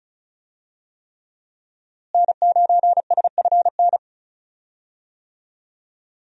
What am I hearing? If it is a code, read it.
Morse code, heard as a keyed tone, about 36 wpm, 700 Hz. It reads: N9SFN
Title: USA Call Signs - 5 Characters - Repeated 3x - 35wpm